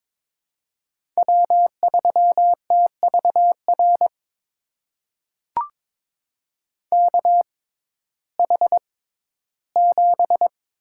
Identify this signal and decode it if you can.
Morse code, heard as a keyed tone, about 22 wpm, spaced as for 14 wpm, 700 Hz. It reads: W3TVR KH7
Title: USA Call Signs - 5 Characters - 6x Character Spacing - 22wpm